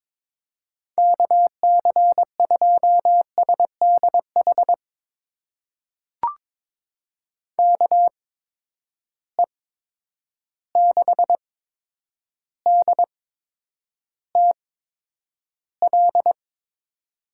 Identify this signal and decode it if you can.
Morse code, heard as a keyed tone, about 22 wpm, spaced as for 9 wpm, 700 Hz. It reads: KC2SDH KE6DTL
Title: USA Call Signs - 6 Characters - 8x Character Spacing - 22wpm